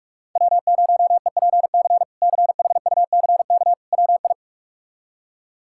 Morse code, heard as a keyed tone, about 43 wpm, 700 Hz. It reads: W0EPC CHUCK WI
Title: K1USN SST QSO Practice - 45wpm